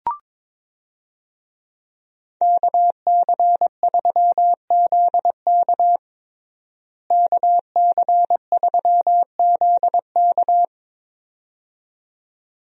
Morse code, heard as a keyed tone, about 22 wpm, 700 Hz. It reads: KC3ZK KC3ZK
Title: USA Call Signs - 5 Characters - Repeated 2x - 22wpm